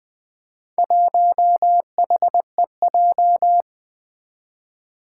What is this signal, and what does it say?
Morse code, heard as a keyed tone, about 20 wpm, 700 Hz. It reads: 1HEJ
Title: Four Challenging Letter-Number Combinations 20wpm